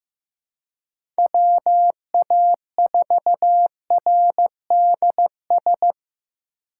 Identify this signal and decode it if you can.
Morse code, heard as a keyed tone, about 15 wpm, 700 Hz. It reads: WA4RDS
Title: USA Call Signs - 6 Characters - 2x Character Spacing - 15wpm